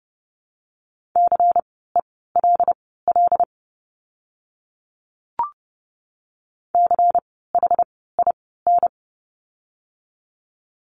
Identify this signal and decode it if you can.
Morse code, heard as a keyed tone, about 30 wpm, spaced as for 17 wpm, 700 Hz. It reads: CELL CHIN
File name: Lesson 16f - Letter c - Words 30wpm